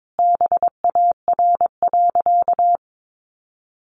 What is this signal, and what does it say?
Morse code, heard as a keyed tone, about 22 wpm, 700 Hz. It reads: BAR.
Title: Additional Easy Phrases 22wpm